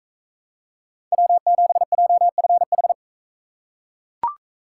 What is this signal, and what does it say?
Morse code, heard as a keyed tone, about 40 wpm, 700 Hz. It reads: W7JFH